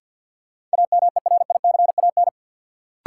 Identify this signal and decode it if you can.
Morse code, heard as a keyed tone, about 48 wpm, 700 Hz. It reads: AMERICAN
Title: Sets of 2 Words - New Word Drill - 50wpm